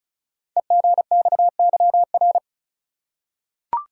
Morse code, heard as a keyed tone, about 36 wpm, 690 Hz. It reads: EGXYR
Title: Code Groups - 5 Letters - 35wpm